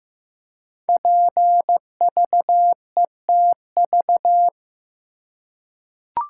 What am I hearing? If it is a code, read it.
Morse code, heard as a keyed tone, about 15 wpm, 700 Hz. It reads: PVETV
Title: Code Groups - 5 Letters - 15wpm